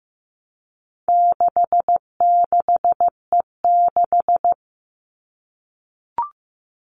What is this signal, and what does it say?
Morse code, heard as a keyed tone, about 15 wpm, 700 Hz. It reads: 66E6
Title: All Cut Numbers - 4-Digit 15wpm